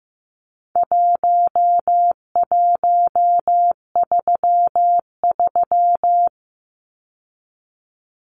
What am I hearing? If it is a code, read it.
Morse code, heard as a keyed tone, about 15 wpm, 700 Hz. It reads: 1133
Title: Numbers - 4-Digit 15wpm